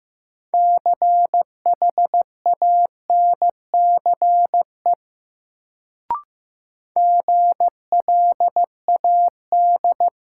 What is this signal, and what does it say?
Morse code, heard as a keyed tone, about 15 wpm, 700 Hz. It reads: CHANCE GLAD
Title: Sentences from Top 800 Words - New Word Drill - 15wpm